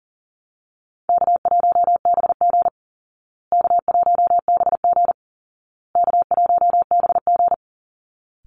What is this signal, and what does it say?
Morse code, heard as a keyed tone, about 40 wpm, 700 Hz. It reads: K1BG K1BG K1BG
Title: Easy Call Signs Repeated 3x - 40wpm